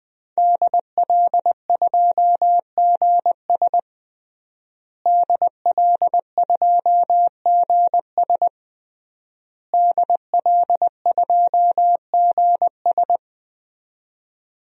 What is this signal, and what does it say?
Morse code, heard as a keyed tone, about 21 wpm, 700 Hz. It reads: DL2GS DL2GS DL2GS
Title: DXCC-100 Call Signs - 5 Characters - Repeated 3x - 20wpm